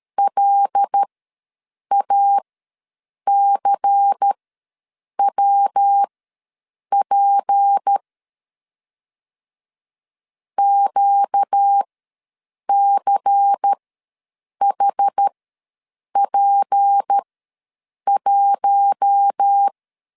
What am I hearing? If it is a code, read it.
Morse code, heard as a keyed tone, about 13 wpm, spaced as for 7 wpm, 800 Hz. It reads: LACWP QCHP1
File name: LEZIONI / 7 GF9Z / 7 GF ws 12wpm 3140